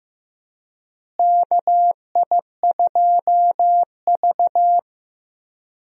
Morse code, heard as a keyed tone, about 15 wpm, 700 Hz. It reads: KI2V
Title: USA Call Signs - 4 Characters - Repeated 3x - 15wpm